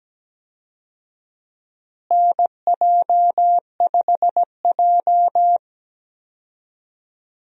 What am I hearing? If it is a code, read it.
Morse code, heard as a keyed tone, about 17 wpm, 700 Hz. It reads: NJ5J
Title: USA Call Signs - 4 Characters - Repeated 3x - 17wpm